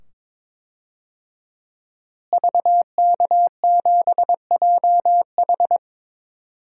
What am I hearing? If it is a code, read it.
Morse code, heard as a keyed tone, about 22 wpm, 700 Hz. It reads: VK7JH
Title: DXCC-100 Call Signs - 5 Characters - Repeated 2x - 22wpm